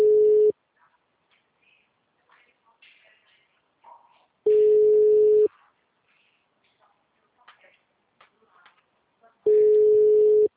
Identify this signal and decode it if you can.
Ringback tone (Europe, 425 Hz), with speech.